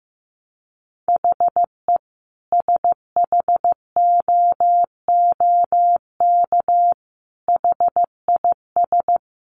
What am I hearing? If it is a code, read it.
Morse code, heard as a keyed tone, about 15 wpm, 700 Hz. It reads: HE SHOOK HIS
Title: A Sound Of Thunder 15wpm